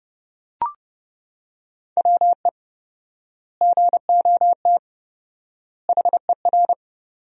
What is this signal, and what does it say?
Morse code, heard as a keyed tone, about 32 wpm, 700 Hz. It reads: WE GOT HER
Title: 4x Word-Spacing - Sentences from Top 400 Words - 30wpm